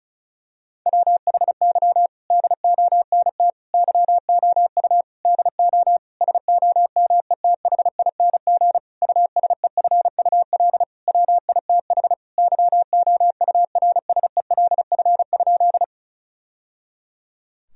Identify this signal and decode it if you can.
Morse code, heard as a keyed tone, about 36 wpm, 700 Hz. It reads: WHY DONT YOU DO SOMETHING USEFUL WITH YOURSELF?
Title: Sentences from Top 1000 Words - Review Part 4 - 35wpm